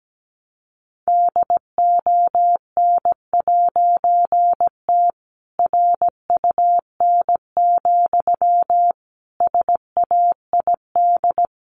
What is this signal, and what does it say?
Morse code, heard as a keyed tone, about 17 wpm, 700 Hz. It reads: DON'T RUN, SAID
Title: A Sound Of Thunder 17wpm